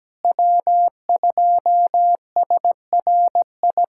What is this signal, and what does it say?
Morse code, heard as a keyed tone, about 17 wpm, 700 Hz. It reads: W2SRI
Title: USA Callsign DE Callsign - 5 Characters - 17wpm